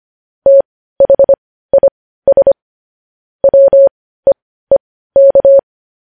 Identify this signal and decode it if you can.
Morse code, heard as a keyed tone, about 25 wpm, spaced as for 15 wpm, 550 Hz. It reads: THIS WEEK
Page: News Headlines in Morse Code at 15 WPM